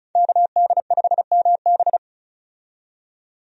Morse code, heard as a keyed tone, about 34 wpm, 700 Hz. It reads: KD5MB
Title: USA Call Signs - 5 Characters - 35wpm